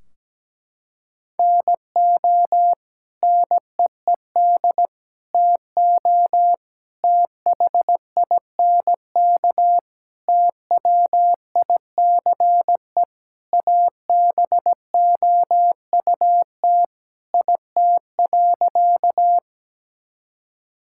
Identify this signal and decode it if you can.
Morse code, heard as a keyed tone, about 17 wpm, 700 Hz. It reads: NO NEED TO THINK TWICE ABOUT IT.
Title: Sentences from Top 1600 Words 17wpm